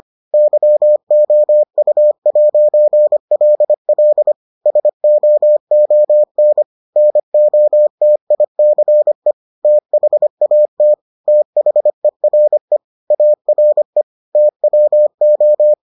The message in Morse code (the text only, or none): YOU'LL SOON NOTICE THAT THERE ARE TWO